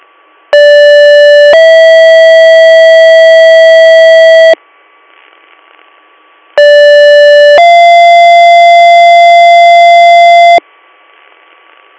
Two-tone page, heard as a one-second tone, then a three-second tone, about 580 then 650 Hz, about 580 then 690 Hz, with noise. It is heard from 0.5 to 10.6 s.